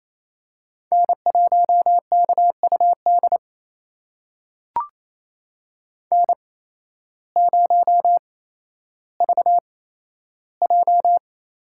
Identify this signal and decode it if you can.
Morse code, heard as a keyed tone, about 28 wpm, spaced as for 13 wpm, 700 Hz. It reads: N1KUD N0VJ
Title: USA Call Signs - 8x Character Spacing - 28wpm